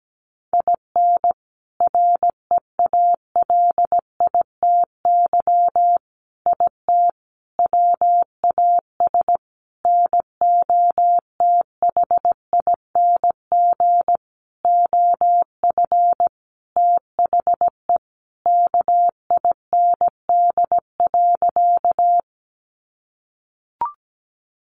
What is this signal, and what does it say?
Morse code, heard as a keyed tone, about 17 wpm, 700 Hz. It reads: IN REALITY IT WAS NOTHING OF THE KIND.
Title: Sentences from Top 1200 Words 17wpm